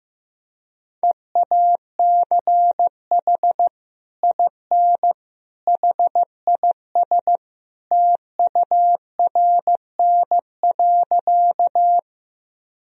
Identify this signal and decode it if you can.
Morse code, heard as a keyed tone, about 15 wpm, 700 Hz. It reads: EACH IN HIS TURN.
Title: Sentences from Top 1000 Words - Review Part 1 - 15wpm